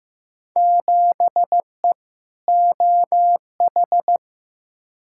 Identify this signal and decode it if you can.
Morse code, heard as a keyed tone, about 15 wpm, 700 Hz. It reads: 7E OH
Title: Field Day Exchanges Encore - 15wpm